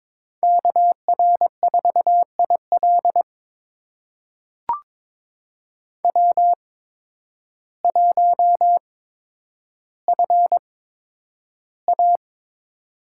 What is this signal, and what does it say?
Morse code, heard as a keyed tone, about 22 wpm, spaced as for 10 wpm, 700 Hz. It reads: KR4IL W1FA
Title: USA Call Signs - 8x Character Spacing - 22wpm